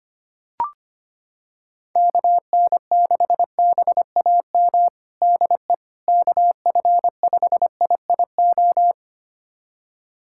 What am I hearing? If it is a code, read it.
Morse code, heard as a keyed tone, about 25 wpm, 700 Hz. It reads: KN6BAM DE KF5IIO
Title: USA Callsign DE Callsign - 6 Characters - 25wpm